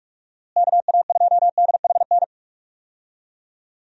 Morse code, heard as a keyed tone, about 46 wpm, 700 Hz. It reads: KA2DHN
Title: USA Call Signs - 6 Characters - Repeated 2x - 45wpm